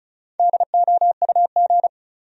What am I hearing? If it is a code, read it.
Morse code, heard as a keyed tone, about 36 wpm, 700 Hz. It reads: DOUG
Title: USA Common Names - 35wpm